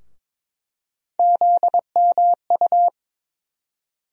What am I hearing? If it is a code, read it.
Morse code, heard as a keyed tone, about 22 wpm, 700 Hz. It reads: ZMU
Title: USA Call Sign Postfixes - 3 Letters 22wpm